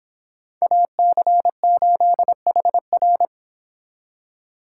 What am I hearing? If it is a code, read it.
Morse code, heard as a keyed tone, about 25 wpm, 700 Hz. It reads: AC8HR